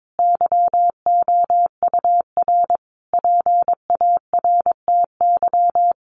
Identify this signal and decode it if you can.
Morse code, heard as a keyed tone, about 22 wpm, 700 Hz. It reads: YOUR PARTY
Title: Sentences from Top 1400 Words 22wpm